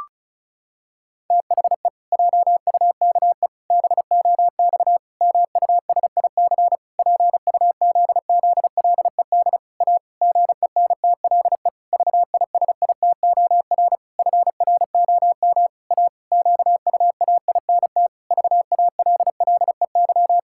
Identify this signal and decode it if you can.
Morse code, heard as a keyed tone, about 34 wpm, 700 Hz. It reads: THE JUKE BOX MUSIC PUZZLED A GENTLE VISITOR FROM A QUAINT VALLEY
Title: Pangram Challenge v2 - 35wpm